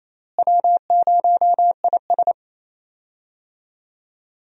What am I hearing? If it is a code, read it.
Morse code, heard as a keyed tone, about 28 wpm, 700 Hz. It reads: W0IS
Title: USA Call Signs - 4 Characters - Repeated 3x - 28wpm